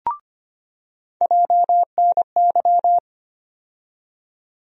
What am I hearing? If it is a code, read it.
Morse code, heard as a keyed tone, about 25 wpm, 700 Hz. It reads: JNY
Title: USA Call Sign Postfixes - 3 Letters 25wpm